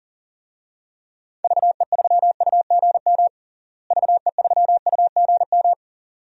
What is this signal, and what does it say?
Morse code, heard as a keyed tone, about 40 wpm, 700 Hz. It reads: VE3UGM VE3UGM
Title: DXCC-100 Call Signs - 6 Characters - Repeated 2x - 40wpm